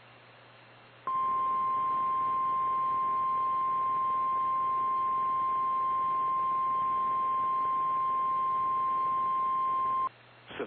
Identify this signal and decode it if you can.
Emergency Alert System (SAME) tones (the 1050 Hz attention tone). It is heard from 1.0 to 10.1 s.
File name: eas / KKPRKXPN / KKPRKXPN